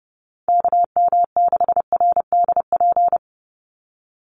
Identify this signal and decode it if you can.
Morse code, heard as a keyed tone, about 30 wpm, 700 Hz. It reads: KM6RDP
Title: USA Call Signs 30wpm